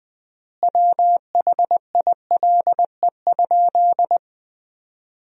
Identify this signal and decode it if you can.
Morse code, heard as a keyed tone, about 21 wpm, 700 Hz. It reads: WHILE?
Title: 3x Word-Spacing - Sentences from Top 300 Words - 20wpm